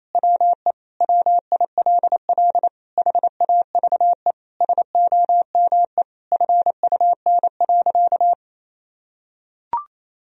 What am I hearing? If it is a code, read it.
Morse code, heard as a keyed tone, about 28 wpm, 700 Hz. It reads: WE WILL HAVE SOME FUN.
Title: Sentences from Top 1000 Words - Review Part 4 - 28wpm